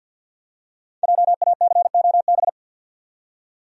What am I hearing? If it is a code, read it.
Morse code, heard as a keyed tone, about 52 wpm, 700 Hz. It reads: JACOB
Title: USA Common Names - Max 5 Letters - 50wpm